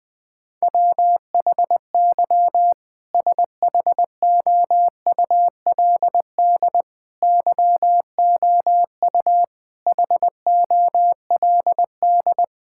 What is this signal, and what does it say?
Morse code, heard as a keyed tone, about 19 wpm, 700 Hz. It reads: WHY SHOULD YOU HOLD
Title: Sentences from Top 1600 Words 20wpm